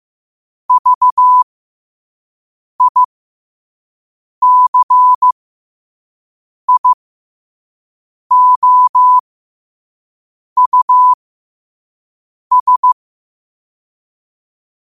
Morse code, heard as a keyed tone, about 15 wpm, spaced as for 5 wpm, 1000 Hz. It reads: VICIOUS
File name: qotdfiles / 5wpm